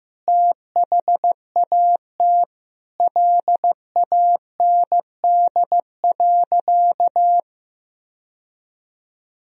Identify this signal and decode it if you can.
Morse code, heard as a keyed tone, about 16 wpm, 700 Hz. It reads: THAT LAND.
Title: Sentences from Top 1000 Words - Review Part 1 - 15wpm